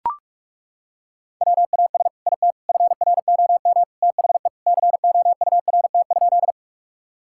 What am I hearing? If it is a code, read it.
Morse code, heard as a keyed tone, about 41 wpm, 700 Hz. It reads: WAS IT FROM THE COURT?
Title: Sentences from Top 1600 Words 45wpm